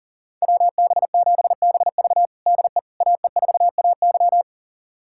Morse code, heard as a keyed tone, about 40 wpm, 700 Hz. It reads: WB7BV DE AE4AY
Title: USA Callsign DE Callsign - 5 Characters - 40wpm